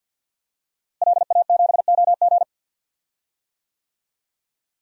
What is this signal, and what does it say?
Morse code, heard as a keyed tone, about 50 wpm, 700 Hz. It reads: LA7OG
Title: DXCC-100 Call Signs - 5 Characters - Repeated 3x - 50wpm